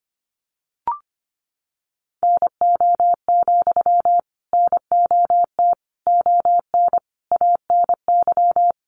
Morse code, heard as a keyed tone, about 25 wpm, 700 Hz. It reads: NO, NOT ON ANY
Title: Sentences from Top 900 Words 25wpm